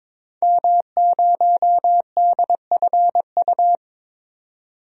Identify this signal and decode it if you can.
Morse code, heard as a keyed tone, about 22 wpm, 700 Hz. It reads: M0DFU
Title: DXCC-100 Call Signs - 5 Characters - 22wpm